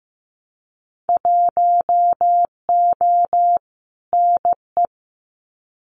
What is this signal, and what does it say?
Morse code, heard as a keyed tone, about 15 wpm, 700 Hz. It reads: 1O NE
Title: Winter Field Day Exchanges Encore 15wpm